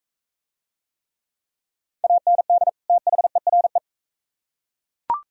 Morse code, heard as a keyed tone, about 39 wpm, 700 Hz. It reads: AND THERE